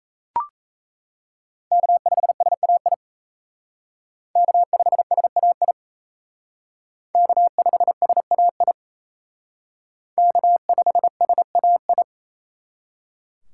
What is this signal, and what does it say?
Morse code, heard as a keyed tone, about 36 wpm, 700 Hz. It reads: K5SAI K5SAI K5SAI K5SAI